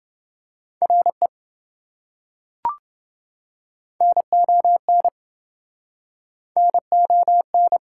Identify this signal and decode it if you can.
Morse code, heard as a keyed tone, about 26 wpm, 700 Hz. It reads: RE NON NON